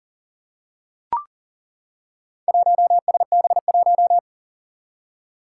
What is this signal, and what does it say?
Morse code, heard as a keyed tone, about 40 wpm, 700 Hz. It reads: 1SB1
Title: Four Challenging Letter-Number Combinations 40wpm